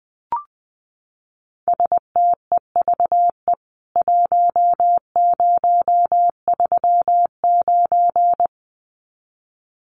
Morse code, heard as a keyed tone, about 19 wpm, 700 Hz. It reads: STEVE 1039
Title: CWT Exchange 20wpm